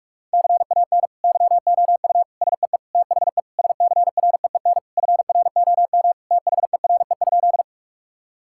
Morse code, heard as a keyed tone, about 46 wpm, 700 Hz. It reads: CAN YOU SEE THE SCREEN FROM THERE?
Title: Sentences from Top 1000 Words - Review Part 6 - 45wpm